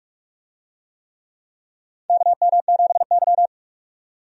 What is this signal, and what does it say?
Morse code, heard as a keyed tone, about 46 wpm, 700 Hz. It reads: KM7Y
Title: USA Call Signs - 4 Characters - 45wpm